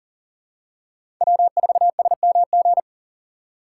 Morse code, heard as a keyed tone, about 40 wpm, 700 Hz. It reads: W4SMG
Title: USA Call Signs - 5 Characters - 4x Character Spacing - 40wpm